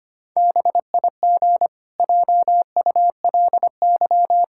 Morse code, heard as a keyed tone, about 25 wpm, 700 Hz. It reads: BIG JULY
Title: Pangram Challenge v2 - 25wpm